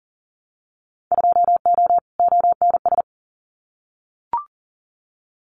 Morse code, heard as a keyed tone, about 40 wpm, 700 Hz. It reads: 2O ONS
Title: Winter Field Day Exchanges Encore 40wpm